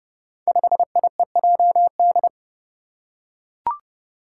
Morse code, heard as a keyed tone, about 32 wpm, 700 Hz. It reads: 5IEJD
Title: Five Challenging Letter-Number Combinations 30wpm